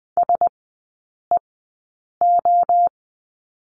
Morse code, heard as a keyed tone, about 20 wpm, 700 Hz. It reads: S E O
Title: 035 Lesson 7e - Letter s - Three Characters 20wpm